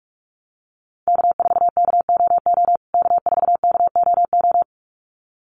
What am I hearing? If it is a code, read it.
Morse code, heard as a keyed tone, about 44 wpm, 700 Hz. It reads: K4KOO K4KOO
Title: USA Call Signs Repeated 2x 45wpm